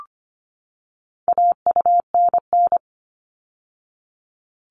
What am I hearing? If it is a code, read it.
Morse code, heard as a keyed tone, about 25 wpm, 700 Hz. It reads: AUNN
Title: All Cut Numbers - 4-Digit 25wpm